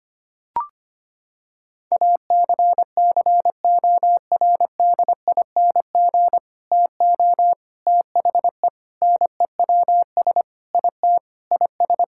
Morse code, heard as a keyed tone, about 25 wpm, 700 Hz. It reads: ACCORDING TO THE NEWS IT IS
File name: practice / 25wpm / Sentences with 3-Word Phrases - 25wpm